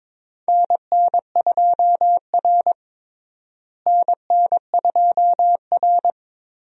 Morse code, heard as a keyed tone, about 22 wpm, 700 Hz. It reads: NN2R NN2R
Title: USA Call Signs - 4 Characters - Repeated 2x - 22wpm